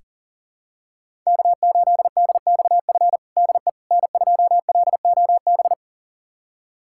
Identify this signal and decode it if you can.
Morse code, heard as a keyed tone, about 40 wpm, 700 Hz. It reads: K8DXF DE N2LOB
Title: USA Callsign DE Callsign - 5 Characters - 40wpm